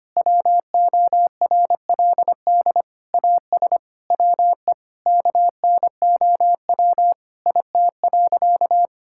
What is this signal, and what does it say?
Morse code, heard as a keyed tone, about 25 wpm, 700 Hz. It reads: WORLD AS WE KNOW IT.
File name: practice / 25wpm / Sentences with 9-Word Phrases - 25wpm